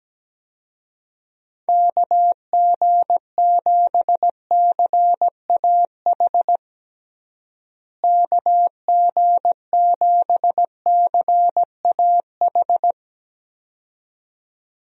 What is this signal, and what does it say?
Morse code, heard as a keyed tone, about 17 wpm, 700 Hz. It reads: KG7CAH KG7CAH
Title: USA Call Signs - 6 Characters - Repeated 2x - 17wpm